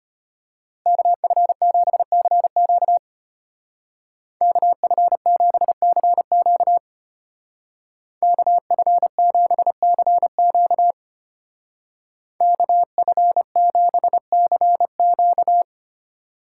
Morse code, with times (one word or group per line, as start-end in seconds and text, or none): 0.9-3.0 KF7CQ
4.4-6.8 KF7CQ
8.2-10.9 KF7CQ
12.4-15.6 KF7CQ